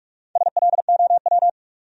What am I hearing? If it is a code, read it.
Morse code, heard as a keyed tone, about 60 wpm, spaced as for 52 wpm, 700 Hz. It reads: SLOW
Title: Speed-Racing Top 500 Words 45wpm